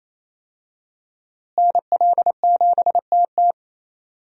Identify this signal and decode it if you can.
Morse code, heard as a keyed tone, about 27 wpm, 700 Hz. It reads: NL7TT
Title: USA Call Signs - 5 Characters - Repeated 2x - 28wpm